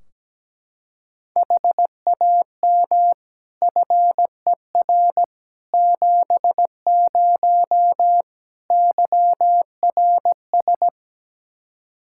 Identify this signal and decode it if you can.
Morse code, heard as a keyed tone, about 17 wpm, 700 Hz. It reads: HAM FER 70 YRS
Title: Practice QSO - Exchange 3 - More Info - 17wpm